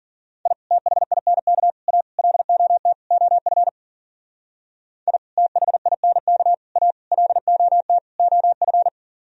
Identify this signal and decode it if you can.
Morse code, heard as a keyed tone, about 53 wpm, 700 Hz. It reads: I THINK A LOT OF I THINK A LOT OF